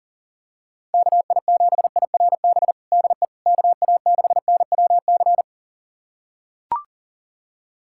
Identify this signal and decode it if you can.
Morse code, heard as a keyed tone, about 40 wpm, 700 Hz. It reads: KI7IRB DE KA6NWC
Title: USA Callsign DE Callsign - 6 Characters - 40wpm